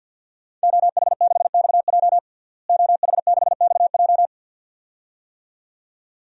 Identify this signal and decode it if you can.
Morse code, heard as a keyed tone, about 52 wpm, 700 Hz. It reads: OH6XJ OH6XJ
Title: DXCC-100 Call Signs - 5 Characters - Repeated 2x - 50wpm